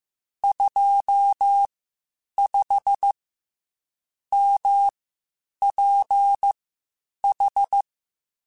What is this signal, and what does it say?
Morse code, heard as a keyed tone, about 15 wpm, spaced as for 8 wpm, 800 Hz. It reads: 25MPH